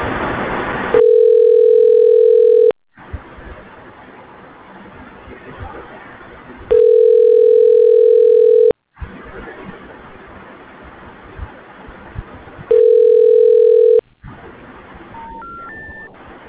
Ringback tone (North America), 440 + 480 Hz.